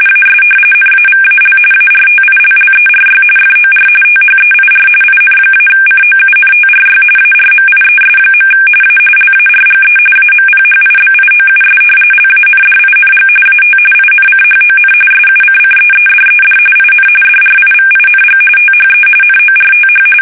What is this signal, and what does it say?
RTTY, about 45.45 baud. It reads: AS NAVAL ARTILLERY FOR THEIR IN-COUNTRY ARMIES. A LARGE AMPHIBIOUS LANDING AT INCHON SUCCEEDED IN DRIVING THE NORTH
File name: fsk850 / usnavy5